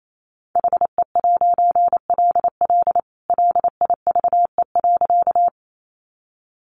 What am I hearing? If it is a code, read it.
Morse code, heard as a keyed tone, about 28 wpm, 700 Hz. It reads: HE'LL LIVE.
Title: A Sound Of Thunder 28wpm